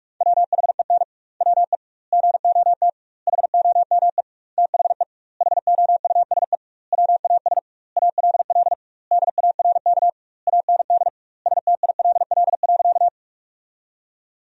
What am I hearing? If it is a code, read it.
Morse code, heard as a keyed tone, about 41 wpm, 700 Hz. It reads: WHEN WE GOT HOME THE HOUSE WAS ALL DARK AND STILL.
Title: 2x Word-Spacing - Sentences from Top 500 Words - 45wpm